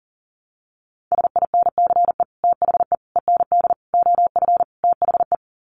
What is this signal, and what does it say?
Morse code, heard as a keyed tone, about 40 wpm, 700 Hz. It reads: SINCE THE END OF THE
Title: Sets of 5 Words Encore 40wpm